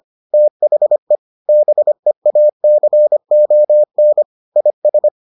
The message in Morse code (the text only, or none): THE BEACON IS